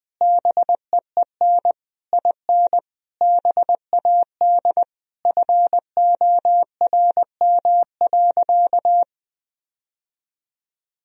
Morse code, heard as a keyed tone, about 20 wpm, 700 Hz. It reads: BEEN IN BAD FORM.
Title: Sentences from Top 1000 Words - Review Part 1 - 20wpm